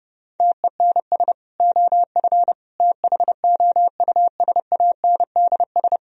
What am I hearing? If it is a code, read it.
Morse code, heard as a keyed tone, about 30 wpm, 700 Hz. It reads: TENS OF THOUSANDS